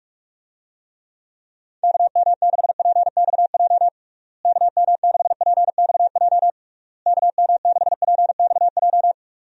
Morse code, heard as a keyed tone, about 46 wpm, 700 Hz. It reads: KM6PXJ KM6PXJ KM6PXJ
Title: USA Call Signs - 6 Characters - Repeated 3x - 45wpm